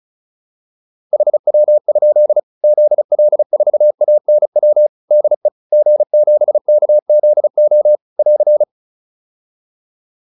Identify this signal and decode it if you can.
Morse code, heard as a keyed tone, about 34 wpm, 600 Hz. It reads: HW? ZL4ANW DE G7KZO <AR>